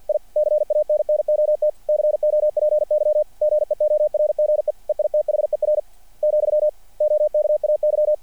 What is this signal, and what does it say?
Morse code, heard as a keyed tone, about 50 wpm, 600 Hz. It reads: I CANNOT COPY GEORGE EITHER , OKAY
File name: archived / CRSnet